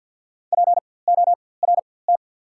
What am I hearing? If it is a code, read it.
Morse code, heard as a keyed tone, about 48 wpm, spaced as for 23 wpm, 690 Hz. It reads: PORT
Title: 4x Character Spacing - Top 300 Words - 50wpm